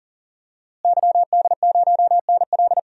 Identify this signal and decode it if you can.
Morse code, heard as a keyed tone, about 40 wpm, 700 Hz. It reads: YD0NL
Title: DXCC-100 Call Signs - 5 Characters - Repeated 2x - 40wpm